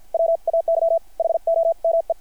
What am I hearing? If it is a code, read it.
Morse code, heard as a keyed tone, about 50 wpm, spaced as for 43 wpm, 660 Hz. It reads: WAY HOME
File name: archived / CRSnet